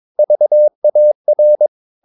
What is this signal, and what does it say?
Morse code, heard as a keyed tone, about 22 wpm, 600 Hz. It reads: VAR
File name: Ferie QRS 22wpm